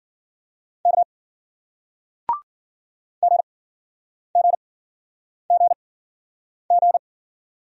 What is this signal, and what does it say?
Morse code, heard as a keyed tone, about 55 wpm, 700 Hz. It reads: K G G G G